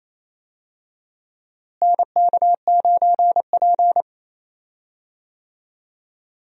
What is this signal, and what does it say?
Morse code, heard as a keyed tone, about 28 wpm, 700 Hz. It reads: NK9P
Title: USA Call Signs - 4 Characters - Repeated 2x - 28wpm